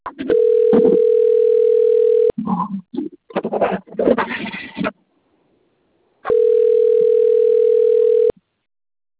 Ringback tone (North America), 440 + 480 Hz.
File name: out- - - - - .3987